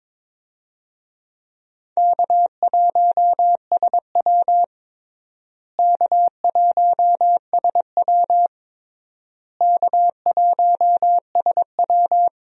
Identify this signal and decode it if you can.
Morse code, heard as a keyed tone, about 23 wpm, 700 Hz. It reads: K1SW K1SW K1SW
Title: USA Call Signs - 4 Characters - Repeated 3x - 22wpm